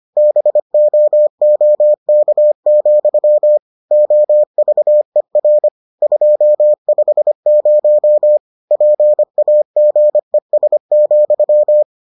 Morse code, heard as a keyed tone, about 25 wpm, 600 Hz. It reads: BOOK, OVER 250 PAGES,